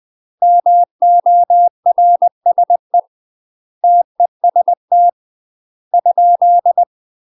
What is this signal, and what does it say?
Morse code, heard as a keyed tone, about 20 wpm, 700 Hz. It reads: MORSE TEST ?
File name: UK MASTER QSOs TRAINING / 20WPM / UK MASTER QSOs 0011